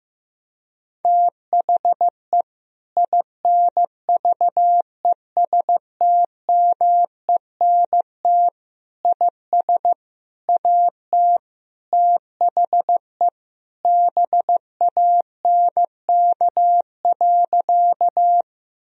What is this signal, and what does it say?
Morse code, heard as a keyed tone, about 15 wpm, 700 Hz. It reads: THE INVESTMENT IS AT THE BANK.
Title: Sentences from Top 1000 Words - Review Part 4 - 15wpm